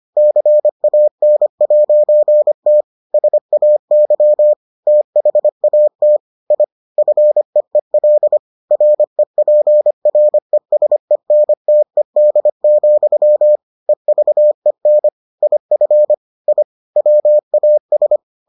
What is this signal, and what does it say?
Morse code, heard as a keyed tone, about 25 wpm, 600 Hz. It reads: CAN'T SAY THAT I FEEL REPRESENTED, EVEN IF I WAS